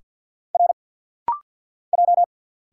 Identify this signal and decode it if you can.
Morse code, heard as a keyed tone, about 50 wpm, 700 Hz. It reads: R J
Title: Single Letters - Rapid-Fire 50wpm